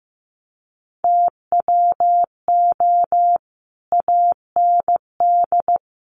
Morse code, heard as a keyed tone, about 15 wpm, 700 Hz. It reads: TWO AND
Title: Sets of 5 Words 15wpm